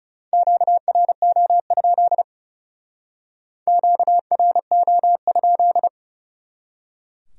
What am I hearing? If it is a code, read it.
Morse code, heard as a keyed tone, about 37 wpm, 700 Hz. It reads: QRO? QRO?
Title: Speed-Racing Q-Codes - 30wpm